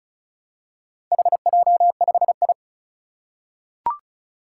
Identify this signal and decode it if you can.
Morse code, heard as a keyed tone, about 36 wpm, 700 Hz. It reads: HJ5I
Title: Four Challenging Letter-Number Combinations 35wpm